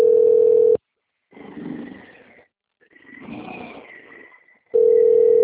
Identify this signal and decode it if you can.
Ringback tone (North America), 440 + 480 Hz.
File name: out- -2053- - - .36868